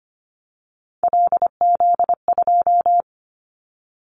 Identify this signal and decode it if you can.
Morse code, heard as a keyed tone, about 25 wpm, 700 Hz. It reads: LZ2
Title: Callsign Prefixes for Top 100 DXCC - 25wpm